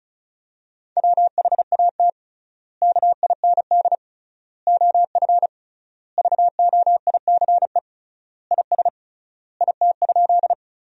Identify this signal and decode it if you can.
Morse code, heard as a keyed tone, about 34 wpm, 700 Hz. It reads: WHAT KIND OF VOICE IS IT?
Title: 3x Word-Spacing - Sentences from Top 500 Words - 35wpm